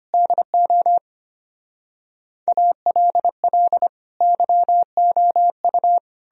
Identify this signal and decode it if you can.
Morse code, heard as a keyed tone, about 33 wpm, 700 Hz. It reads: DO ALL YOU